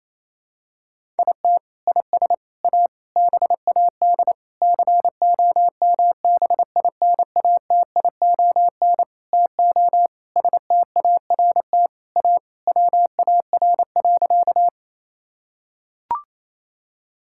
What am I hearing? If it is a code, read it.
Morse code, heard as a keyed tone, about 28 wpm, 700 Hz. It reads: IT IS A BAD COMBINATION TO START A WAR.
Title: Sentences from Top 1000 Words - Review Part 6 - 28wpm